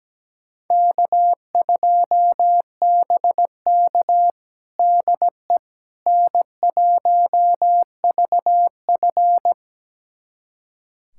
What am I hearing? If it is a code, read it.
Morse code, heard as a keyed tone, about 17 wpm, 700 Hz. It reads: K2BK DE N1VF
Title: USA Callsign DE Callsign - 4 Characters - 17wpm